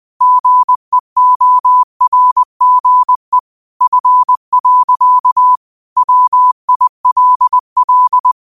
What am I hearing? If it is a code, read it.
Morse code, heard as a keyed tone, about 20 wpm, 1000 Hz. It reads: GEORGE F. WILL
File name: qotdfiles / 20wpm